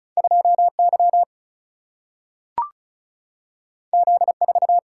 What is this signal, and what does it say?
Morse code, heard as a keyed tone, about 36 wpm, 700 Hz. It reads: 2Y Z4
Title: Single-Letter Single-Number 35wpm